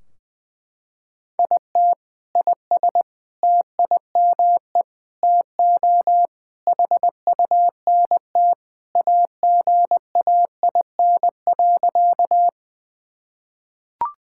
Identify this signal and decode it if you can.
Morse code, heard as a keyed tone, about 20 wpm, 700 Hz. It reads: IT IS TIME TO HUNT AGAIN.
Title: Sentences from Top 1600 Words 20wpm